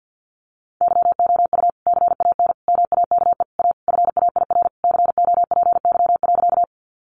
Morse code, heard as a keyed tone, about 50 wpm, 700 Hz. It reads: YOU CAN MAKE A FAIR COPY.
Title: Sentences from Top 1400 Words 50wpm